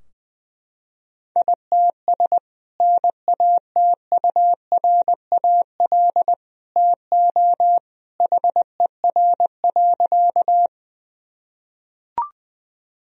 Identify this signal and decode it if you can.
Morse code, heard as a keyed tone, about 21 wpm, 700 Hz. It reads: ITS NATURAL TO HER.
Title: Sentences from Top 1000 Words - Review Part 4 - 20wpm